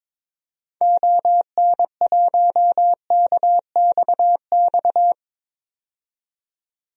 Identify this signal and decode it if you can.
Morse code, heard as a keyed tone, about 22 wpm, 700 Hz. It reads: ON1KXX
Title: DXCC-100 Call Signs - 6 Characters - 22wpm